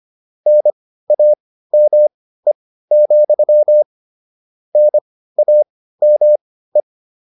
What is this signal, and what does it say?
Morse code, heard as a keyed tone, about 25 wpm, spaced as for 15 wpm, 600 Hz. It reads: NAME, NAME
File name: QSO403 15